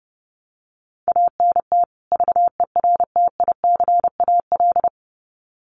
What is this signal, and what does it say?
Morse code, heard as a keyed tone, about 30 wpm, 700 Hz. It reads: ANT VERTICAL
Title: Common QSO Phrases 30wpm